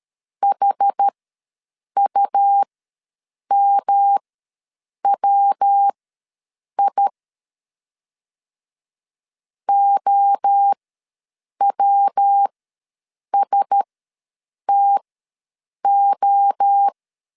Morse code, heard as a keyed tone, about 13 wpm, spaced as for 7 wpm, 800 Hz. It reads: HUMWI OWSTO
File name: 5 LB ws 12wpm